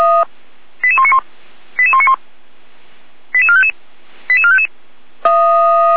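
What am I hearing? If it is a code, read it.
Five-tone selective call (ZVEI), heard as quick runs of five tones, tones of about 80 ms. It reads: 88191 88191 88455 88455
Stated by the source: ZVEI five-tone selective call received via SDR, from the Feuerwehr Orlamünde audio server.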